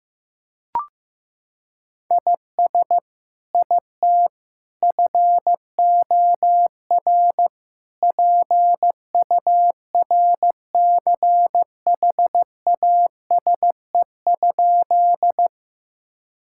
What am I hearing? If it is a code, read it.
Morse code, heard as a keyed tone, about 15 wpm, 700 Hz. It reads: IS IT FOR PURCHASE?
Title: Sentences from Top 1000 Words - Review Part 5 - 15wpm